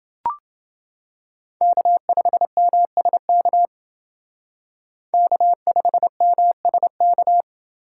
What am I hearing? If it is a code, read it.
Morse code, heard as a keyed tone, about 29 wpm, 700 Hz. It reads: K5MSK K5MSK